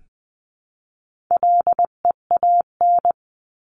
Morse code, heard as a keyed tone, about 20 wpm, 700 Hz. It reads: LEAN
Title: 259 Lesson 42d - Words 18-20wpm